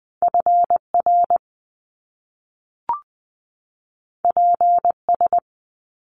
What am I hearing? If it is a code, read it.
Morse code, heard as a keyed tone, about 20 wpm, 700 Hz. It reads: FR PS
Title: Top Two Letter Combinations in Words 20wpm